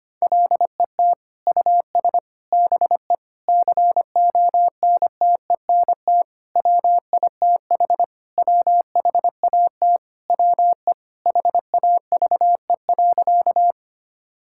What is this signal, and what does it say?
Morse code, heard as a keyed tone, about 25 wpm, 700 Hz. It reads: LET US BE CONTENT WITH WHAT WE HAVE.
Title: Sentences from Top 1000 Words - Review Part 6 - 25wpm